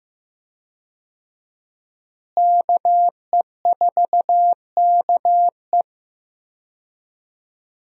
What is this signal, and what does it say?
Morse code, heard as a keyed tone, about 15 wpm, 700 Hz. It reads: KE4KE
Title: USA Call Signs - 5 Characters - Repeated 2x - 15wpm